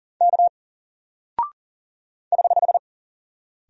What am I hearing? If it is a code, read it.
Morse code, heard as a keyed tone, about 40 wpm, 700 Hz. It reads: K <HH>
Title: Single Character (Letter:Number:Punctuation:Prosign) Alt 40wpm